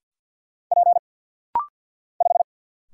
Morse code, heard as a keyed tone, about 48 wpm, 700 Hz. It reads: P 5
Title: Single Letter-Number - Rapid-Fire 50wpm